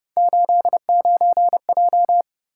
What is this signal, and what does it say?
Morse code, heard as a keyed tone, about 30 wpm, 700 Hz. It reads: 89J